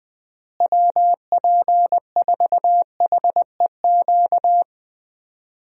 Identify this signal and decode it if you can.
Morse code, heard as a keyed tone, about 20 wpm, 700 Hz. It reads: WP4HEQ
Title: USA Call Signs - 6 Characters - Repeated 2x - 20wpm